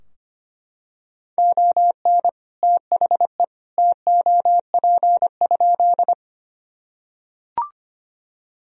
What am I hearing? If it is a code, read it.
Morse code, heard as a keyed tone, about 25 wpm, 700 Hz. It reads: ON THE TOP?
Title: Sentences from Top 1000 Words - Review Part 2 - 25wpm